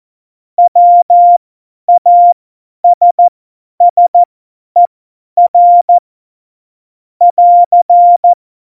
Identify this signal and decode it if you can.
Morse code, heard as a keyed tone, about 14 wpm, spaced as for 10 wpm, 700 Hz. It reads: WASSER <AR>
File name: Morsekurs / Witze / Witz 19. November 2019, 50 F70